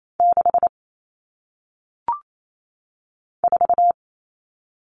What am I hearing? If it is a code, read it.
Morse code, heard as a keyed tone, about 28 wpm, 700 Hz. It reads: 6 4
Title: All Cut Numbers - 1-Digit 28wpm